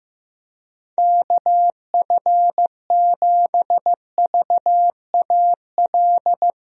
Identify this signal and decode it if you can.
Morse code, heard as a keyed tone, about 15 wpm, 700 Hz. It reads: KF7VAL
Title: USA Call Signs - 6 Characters - 15wpm